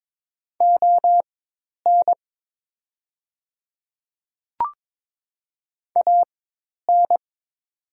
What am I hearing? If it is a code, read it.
Morse code, heard as a keyed tone, about 23 wpm, spaced as for 10 wpm, 700 Hz. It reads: ON AN
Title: 4x Character Spacing - Top 100 Words - 22wpm